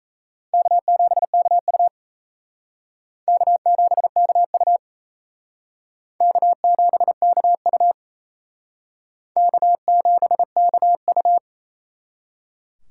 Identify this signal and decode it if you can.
Morse code, heard as a keyed tone, about 38 wpm, 700 Hz. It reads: K7KU K7KU K7KU K7KU